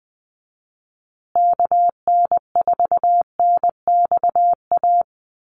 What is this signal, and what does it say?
Morse code, heard as a keyed tone, about 20 wpm, 700 Hz. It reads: KN4NXA
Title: USA Call Signs Repeated 3x 20wpm